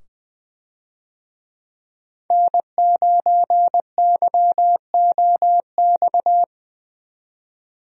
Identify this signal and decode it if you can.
Morse code, heard as a keyed tone, about 20 wpm, 700 Hz. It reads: N9YOX
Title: USA Call Signs - 5 Characters - 20wpm